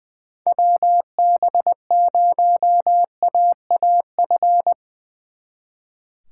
Morse code, heard as a keyed tone, about 22 wpm, 700 Hz. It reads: WB0AAF
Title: Speed Racing - USA 2x3 Call Signs 20wpm